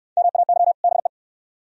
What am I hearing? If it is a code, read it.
Morse code, heard as a keyed tone, about 66 wpm, 690 Hz. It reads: MAY BE